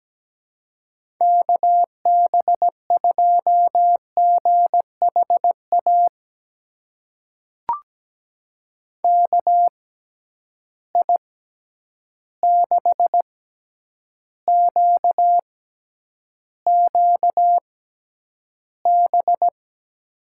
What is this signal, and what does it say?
Morse code, heard as a keyed tone, about 17 wpm, spaced as for 9 wpm, 700 Hz. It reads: KB2GHA KI6QQB
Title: USA Call Signs - 6 Characters - 6x Character Spacing - 17wpm